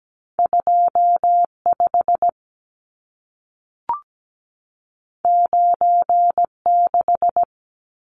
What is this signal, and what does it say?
Morse code, heard as a keyed tone, about 17 wpm, 700 Hz. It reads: 25 96
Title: Numbers - 2-Digit 17wpm